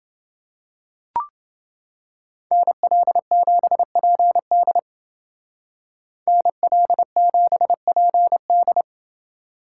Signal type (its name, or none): Morse code